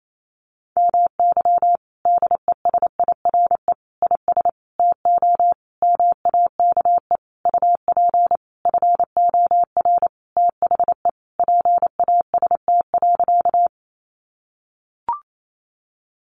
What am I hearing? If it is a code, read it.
Morse code, heard as a keyed tone, about 28 wpm, 700 Hz. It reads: MY DESIRE IS TO MAKE UP FOR THE PAST.
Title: Sentences from Top 1100 Words 28wpm